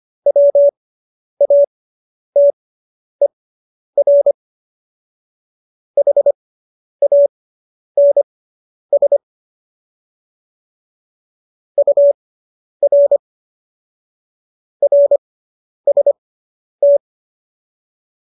Morse code, heard as a keyed tone, about 25 wpm, spaced as for 10 wpm, 570 Hz. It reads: WATER HANS UR RST